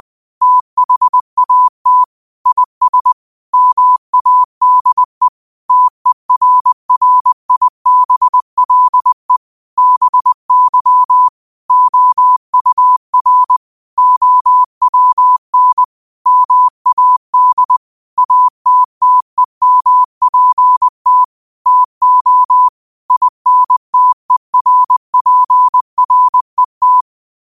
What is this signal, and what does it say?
Morse code, heard as a keyed tone, about 20 wpm, 1000 Hz. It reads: THAT IS MADE TERRIBLE BY OUR OWN MAD ATTEMPT TO INTERPRET
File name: qotdfiles / 20wpm